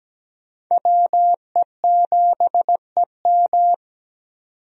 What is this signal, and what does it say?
Morse code, heard as a keyed tone, about 17 wpm, 700 Hz. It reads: WE7EM
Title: USA Call Signs - 5 Characters - 17wpm